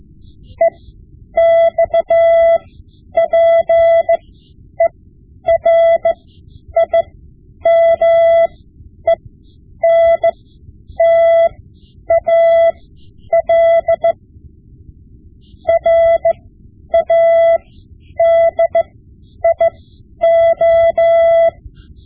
Morse code, heard as a keyed tone, about 15 wpm, spaced as for 9 wpm, 670 Hz. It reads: EXPERIMENTAL RADIO
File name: Morse Lessons / Morse Lessons / Tape-5B